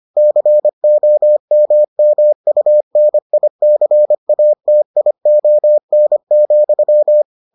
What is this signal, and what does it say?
Morse code, heard as a keyed tone, about 25 wpm, 600 Hz. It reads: COMMUNICATION,